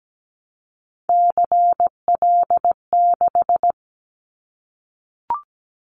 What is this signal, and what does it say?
Morse code, heard as a keyed tone, about 17 wpm, 700 Hz. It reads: CL6
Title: Callsign Prefixes for Top 100 DXCC - 17wpm